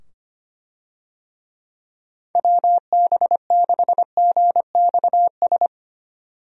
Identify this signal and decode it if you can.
Morse code, heard as a keyed tone, about 25 wpm, 700 Hz. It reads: WB6GXS
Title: USA Call Signs - 6 Characters - 25wpm